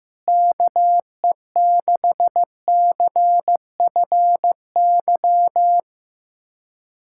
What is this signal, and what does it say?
Morse code, heard as a keyed tone, about 15 wpm, 700 Hz. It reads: KE6CFY
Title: USA Callsign DE Callsign 15wpm